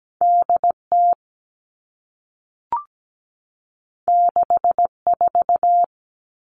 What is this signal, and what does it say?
Morse code, heard as a keyed tone, about 17 wpm, 700 Hz. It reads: DT 64
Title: All Cut Numbers - 2-Digit 17wpm